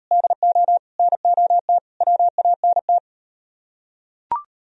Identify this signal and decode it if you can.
Morse code, heard as a keyed tone, about 36 wpm, 700 Hz. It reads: DO NOT WANT